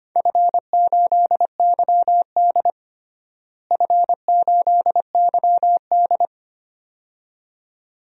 Morse code, heard as a keyed tone, about 25 wpm, 700 Hz. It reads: F8YD F8YD
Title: DXCC-100 Call Signs - 4 Characters - Repeated 2x - 25wpm